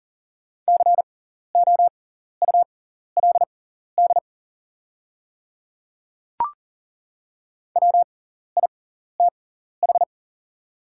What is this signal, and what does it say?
Morse code, heard as a keyed tone, about 40 wpm, spaced as for 14 wpm, 700 Hz. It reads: COULD WITH
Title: 6x Character Spacing - Top 100 Words - 40wpm